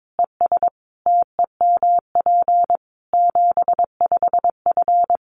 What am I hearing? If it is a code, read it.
Morse code, heard as a keyed tone, about 22 wpm, 700 Hz. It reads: ES TEMP 75F
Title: Common QSO Phrases 22wpm